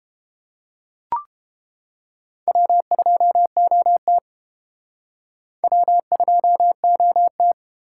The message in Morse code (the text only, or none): W2OT W2OT